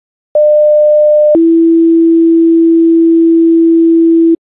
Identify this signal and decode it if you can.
Two-tone page, heard as a one-second tone, then a three-second tone, about 600 then 330 Hz. It is heard from 0.3 to 4.4 s.